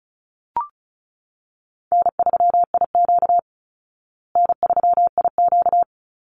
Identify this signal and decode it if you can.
Morse code, heard as a keyed tone, about 34 wpm, 700 Hz. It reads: N3IQ N3IQ
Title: Easy Call Signs Repeated 2x - 35wpm